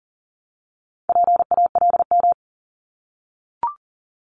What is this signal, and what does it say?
Morse code, heard as a keyed tone, about 42 wpm, 690 Hz. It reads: PALM
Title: Multi-letter Phonemes Repeated 3x 40wpm